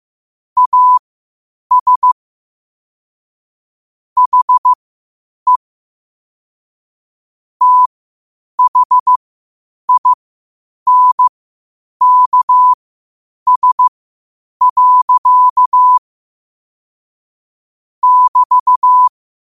Morse code, heard as a keyed tone, about 15 wpm, spaced as for 8 wpm, 1000 Hz. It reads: AS HE THINKS. <BT>
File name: qotdfiles / 8wpm